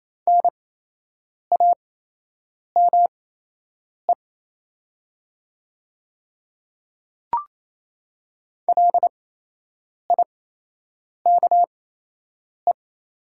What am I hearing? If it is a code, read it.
Morse code, heard as a keyed tone, about 28 wpm, spaced as for 8 wpm, 700 Hz. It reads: NAME LIKE
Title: 8x Character Spacing - Top 300 Words - 28wpm